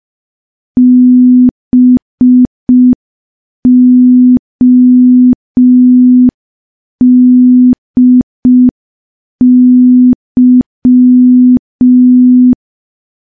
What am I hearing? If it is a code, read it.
Morse code, heard as a keyed tone, about 5 wpm, 250 Hz. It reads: BODY
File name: morse / morse-training-05WPM